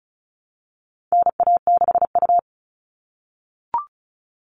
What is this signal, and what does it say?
Morse code, heard as a keyed tone, about 34 wpm, 700 Hz. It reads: NA6U
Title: All Cut Numbers - 4-Digit 35wpm